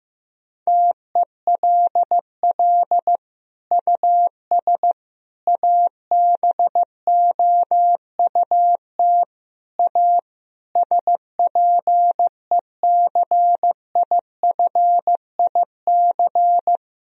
Morse code, heard as a keyed tone, about 15 wpm, 700 Hz. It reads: TELL US ABOUT A SPECIFIC
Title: Sentences from Top 1000 Words - Review Part 3 - 15wpm